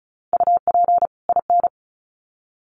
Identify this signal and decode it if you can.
Morse code, heard as a keyed tone, about 34 wpm, 700 Hz. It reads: UP IN
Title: Sets of 2 Words 35wpm